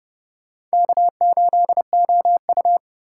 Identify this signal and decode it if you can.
Morse code, heard as a keyed tone, about 30 wpm, 700 Hz. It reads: K8OU
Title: USA Call Signs - 4 Characters - 2x Character Spacing - 30wpm